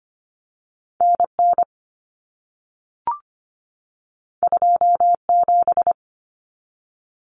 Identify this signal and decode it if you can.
Morse code, heard as a keyed tone, about 25 wpm, 700 Hz. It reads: N N 2 7
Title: Common Cut Numbers - 2-Digit 25wpm